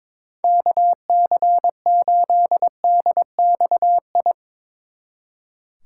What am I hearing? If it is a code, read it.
Morse code, heard as a keyed tone, about 23 wpm, 700 Hz. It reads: KC8DXI
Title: Speed Racing - USA 2x3 Call Signs 22wpm